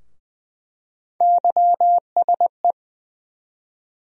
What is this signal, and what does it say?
Morse code, heard as a keyed tone, about 20 wpm, 700 Hz. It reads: YSE
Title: USA Call Sign Postfixes - 3 Letters 20wpm